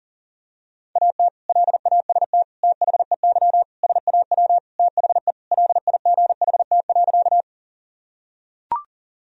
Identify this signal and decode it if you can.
Morse code, heard as a keyed tone, about 40 wpm, 700 Hz. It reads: AT LAST THEY SAW THE LIGHT.
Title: Sentences from Top 1000 Words - Review Part 1 - 40wpm